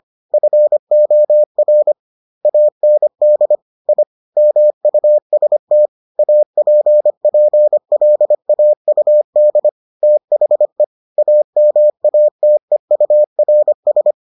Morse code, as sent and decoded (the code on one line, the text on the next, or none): ..-. --- .-. / .- -. -.. / .. / -- ..- ... - / .- .--. .--. .-.. .- ..- -.. / - .... / .- -- .- - . ..- .-. ...
FOR AND I MUST APPLAUD THE AMATEURS